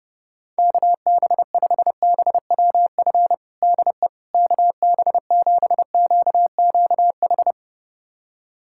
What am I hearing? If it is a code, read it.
Morse code, heard as a keyed tone, about 30 wpm, 700 Hz. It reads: KB5BWF DE KB7QQH
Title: USA Callsign DE Callsign 30wpm